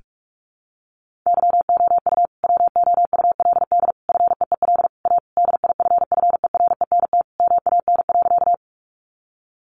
Morse code, heard as a keyed tone, about 44 wpm, 700 Hz. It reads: YOU WOULD FEEL A DIFFERENT MAN.
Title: Sentences from Top 600 Words 45wpm